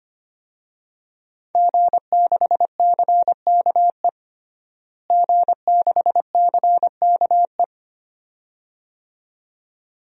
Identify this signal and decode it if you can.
Morse code, heard as a keyed tone, about 25 wpm, 700 Hz. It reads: G6CKE G6CKE
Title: DXCC-100 Call Signs - 5 Characters - Repeated 2x - 25wpm